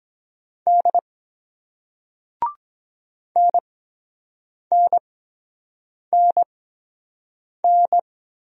Morse code, heard as a keyed tone, about 23 wpm, spaced as for 7 wpm, 700 Hz. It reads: D NNNN